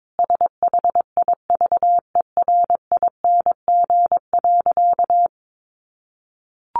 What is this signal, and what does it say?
Morse code, heard as a keyed tone, about 22 wpm, 700 Hz. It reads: SHIVERING.
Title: A Sound Of Thunder 22wpm